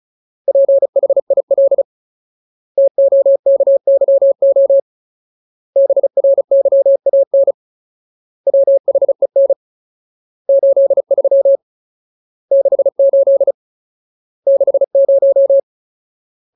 Morse code, heard as a keyed tone, about 34 wpm, 550 Hz. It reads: PHIL TOKYO BRYAN WHEN 83 68 60
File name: Ultimate Word List / 35WPM Ultimate